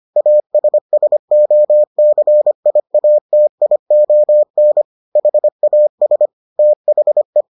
Morse code, heard as a keyed tone, about 25 wpm, 600 Hz. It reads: ASSOCIATION HAS THE